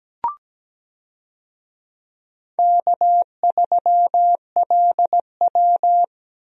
Morse code, heard as a keyed tone, about 17 wpm, 700 Hz. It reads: K3LW
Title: USA Call Signs - 4 Characters - Repeated 3x - 17wpm